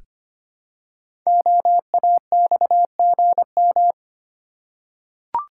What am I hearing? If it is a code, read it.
Morse code, heard as a keyed tone, about 25 wpm, 700 Hz. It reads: OAXGM
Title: Code Groups - 5 Letters - 25wpm